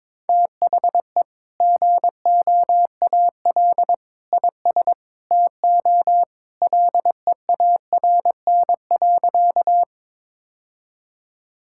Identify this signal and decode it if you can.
Morse code, heard as a keyed tone, about 22 wpm, 700 Hz. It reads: THE GOAL IS TO LEARN.
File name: practice / 22wpm / Sentences with 2-Word Phrases - 22wpm